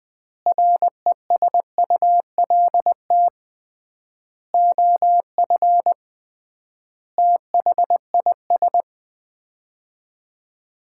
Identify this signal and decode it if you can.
Morse code, heard as a keyed tone, about 20 wpm, 700 Hz. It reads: RESULT OF THIS
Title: 3x Word-Spacing - Sets of 5 Words - 20wpm